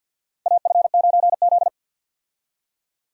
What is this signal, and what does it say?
Morse code, heard as a keyed tone, about 48 wpm, 700 Hz. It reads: AF9Z
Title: USA Call Signs - 4 Characters - 50wpm